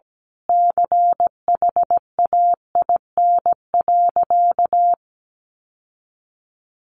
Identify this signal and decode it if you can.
Morse code, heard as a keyed tone, about 17 wpm, 700 Hz. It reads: CHAIN.
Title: Sentences from Top 1500 Words 17wpm